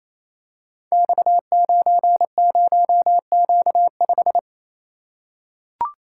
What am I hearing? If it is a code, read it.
Morse code, heard as a keyed tone, about 28 wpm, 700 Hz. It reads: X90Q5
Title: Code Groups - 5 Letter:Numbers - 28wpm